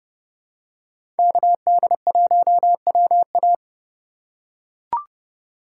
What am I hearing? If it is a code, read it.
Morse code, heard as a keyed tone, about 31 wpm, 700 Hz. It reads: KD1WA
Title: USA Call Signs - 5 Characters - 2x Character Spacing - 30wpm